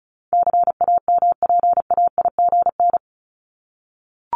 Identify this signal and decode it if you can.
Morse code, heard as a keyed tone, about 34 wpm, 700 Hz. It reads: CAMPAIGN
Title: Top 1001-1250 Words 35wpm